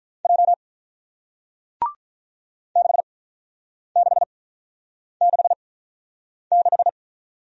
Morse code, heard as a keyed tone, about 46 wpm, spaced as for 9 wpm, 700 Hz. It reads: J 6666